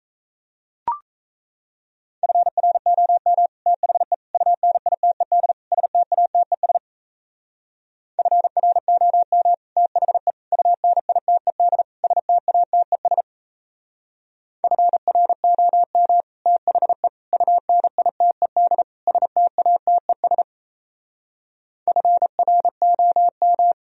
Morse code, heard as a keyed tone, about 37 wpm, 700 Hz. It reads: FROM THE UNITED STATES FROM THE UNITED STATES FROM THE UNITED STATES FROM